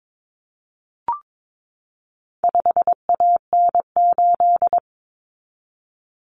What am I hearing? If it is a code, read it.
Morse code, heard as a keyed tone, about 22 wpm, 700 Hz. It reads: 5AN8
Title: Common Cut Numbers - 4-Digit 22wpm